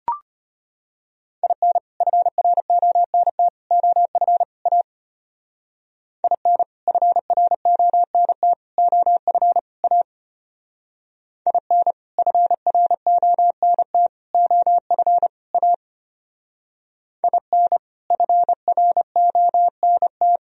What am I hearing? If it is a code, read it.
Morse code, heard as a keyed tone, about 30 wpm, 700 Hz. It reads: IN FRONT OF A IN FRONT OF A IN FRONT OF A IN FRONT
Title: Speed-Racing - Sets of 4 Words 25wpm